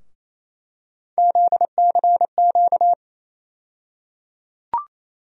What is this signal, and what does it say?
Morse code, heard as a keyed tone, about 28 wpm, 700 Hz. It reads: ZCQ
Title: USA Call Sign Postfixes - 3 Letters 28wpm